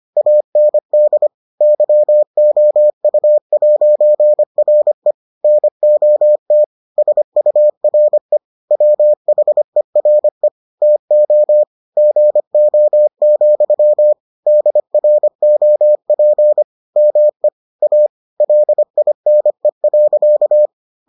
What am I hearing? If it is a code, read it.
Morse code, heard as a keyed tone, about 25 wpm, 600 Hz. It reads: AND YOU'RE NOT SURE WHERE TO GO, DROP ME A LINE.